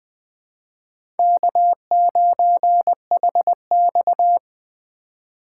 Morse code, heard as a keyed tone, about 21 wpm, 700 Hz. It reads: K9HX
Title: Speed Racing - USA 1x2 Call Signs 17wpm